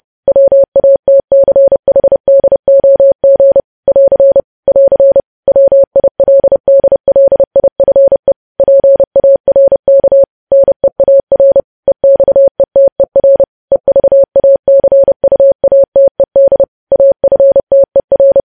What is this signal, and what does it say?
Morse code, heard as a keyed tone, about 30 wpm, 550 Hz. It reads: WATCHDOG <AR> <AR> WILDLIFE PARK NEAR EXETER EVACUATED AFTER
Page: news headlines in morse code at 30 wpm